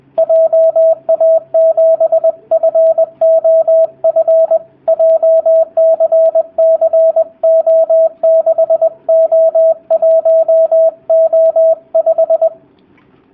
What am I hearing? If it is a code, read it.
Morse code, heard as a keyed tone, about 20 wpm, 650 Hz, with radio noise. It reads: JA7FOF JCCO6O1O5
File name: jh8cbh / 2019cw / sp20- w